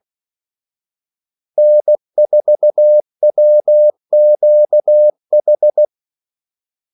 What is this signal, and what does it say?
Morse code, heard as a keyed tone, about 16 wpm, 600 Hz. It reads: N4WQH
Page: Kaldesignaler 16 wpm